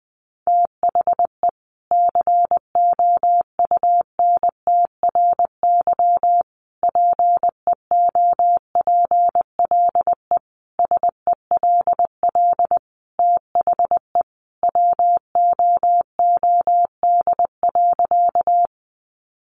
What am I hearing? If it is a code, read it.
Morse code, heard as a keyed tone, about 20 wpm, 700 Hz. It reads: THE COUNTRY PEOPLE SELL THE WOOD.
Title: Sentences from Top 1100 Words 20wpm